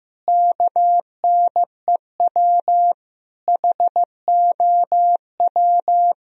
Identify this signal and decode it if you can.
Morse code, heard as a keyed tone, about 15 wpm, 700 Hz. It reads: KNEW HOW
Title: Sentences from Top 1000 Words - Review Part 1 - 15wpm